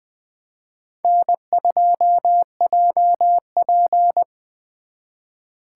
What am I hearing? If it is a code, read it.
Morse code, heard as a keyed tone, about 24 wpm, 700 Hz. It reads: N2JP
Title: Speed Racing - USA 1x2 Call Signs 20wpm